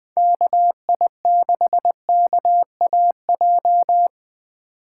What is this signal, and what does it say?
Morse code, heard as a keyed tone, about 21 wpm, 700 Hz. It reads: KI6KAJ
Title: USA Call Signs - 6 Characters - 2x Character Spacing - 20wpm